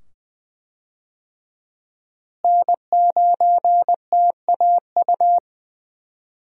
Morse code, heard as a keyed tone, about 21 wpm, 700 Hz. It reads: N9TAU
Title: USA Call Signs - 5 Characters - Repeated 2x - 20wpm